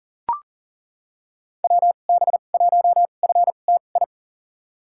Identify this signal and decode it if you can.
Morse code, heard as a keyed tone, about 38 wpm, spaced as for 28 wpm, 700 Hz. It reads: WB1FTI
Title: USA Call Signs - 6 Characters - 2x Character Spacing - 40wpm